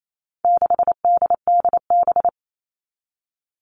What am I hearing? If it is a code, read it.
Morse code, heard as a keyed tone, about 28 wpm, 700 Hz. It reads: 6DDB
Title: All Cut Numbers - 4-Digit 28wpm